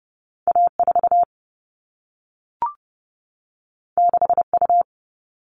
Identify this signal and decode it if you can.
Morse code, heard as a keyed tone, about 30 wpm, 700 Hz. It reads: A4 6U
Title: All Cut Numbers - 2-Digit 30wpm